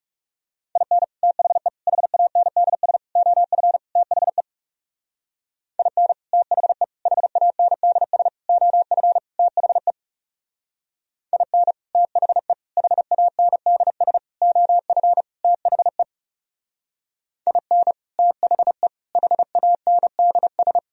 Morse code, heard as a keyed tone, about 36 wpm, 700 Hz. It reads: IN THE HANDS OF THE IN THE HANDS OF THE IN THE HANDS OF THE IN THE HANDS
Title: Speed-Racing - Sets of 5 Words 30wpm